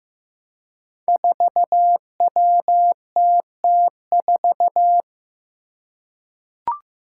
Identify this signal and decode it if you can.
Morse code, heard as a keyed tone, about 15 wpm, 700 Hz. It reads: 4WTT4
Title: Five Challenging Letter-Number Combinations 15wpm